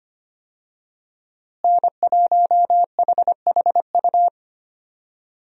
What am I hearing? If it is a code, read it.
Morse code, heard as a keyed tone, about 25 wpm, 700 Hz. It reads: N1HHU
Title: USA Call Signs - 5 Characters - Repeated 3x - 25wpm